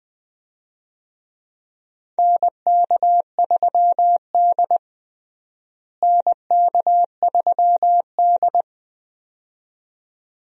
Morse code, heard as a keyed tone, about 20 wpm, 700 Hz. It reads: NK3D NK3D
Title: USA Call Signs - 4 Characters - Repeated 2x - 20wpm